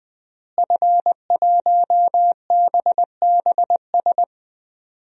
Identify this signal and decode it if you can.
Morse code, heard as a keyed tone, about 20 wpm, 700 Hz. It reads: F1BBS
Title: DXCC-100 Call Signs - 5 Characters - Repeated 3x - 20wpm